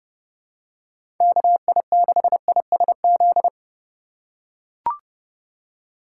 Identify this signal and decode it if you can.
Morse code, heard as a keyed tone, about 31 wpm, 700 Hz. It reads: KI6ISZ
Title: USA Call Signs - 6 Characters - 4x Character Spacing - 30wpm